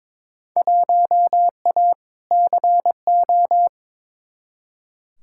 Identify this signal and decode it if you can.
Morse code, heard as a keyed tone, about 22 wpm, 700 Hz. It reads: 1A CO
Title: Field Day Exchanges - 22wpm